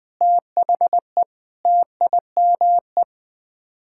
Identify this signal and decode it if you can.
Morse code, heard as a keyed tone, about 20 wpm, 700 Hz. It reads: THE TIME
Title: Sets of 9 Words - 20wpm